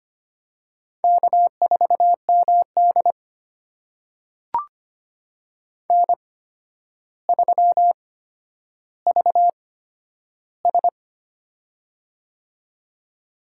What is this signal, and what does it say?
Morse code, heard as a keyed tone, about 25 wpm, spaced as for 11 wpm, 700 Hz. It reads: K4MD N3VS
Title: USA Call Signs - 4 Characters - 8x Character Spacing - 25wpm